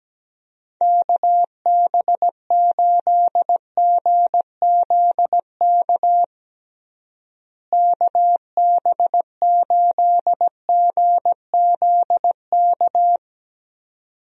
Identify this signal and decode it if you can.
Morse code, heard as a keyed tone, about 17 wpm, 700 Hz. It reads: KB8GZK KB8GZK
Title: USA Call Signs - 6 Characters - Repeated 2x - 17wpm